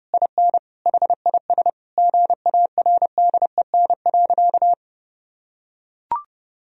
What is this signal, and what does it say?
Morse code, heard as a keyed tone, about 30 wpm, 700 Hz. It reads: IN HIS GARDEN.
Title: Sentences from Top 1000 Words - Review Part 4 - 30wpm